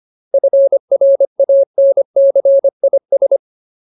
Morse code, heard as a keyed tone, about 25 wpm, 550 Hz. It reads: FRANCIS
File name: Ultimate Word List / 25WPM Ultimate